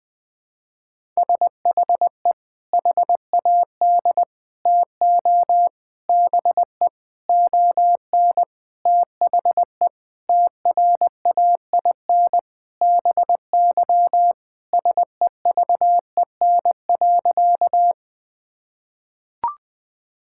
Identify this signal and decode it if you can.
Morse code, heard as a keyed tone, about 20 wpm, 700 Hz. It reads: SHE HAD TO BE ON THE TRAIN BY SEVEN.
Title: Sentences with 4-Word Phrases - 20wpm